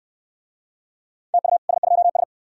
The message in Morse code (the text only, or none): TU S2S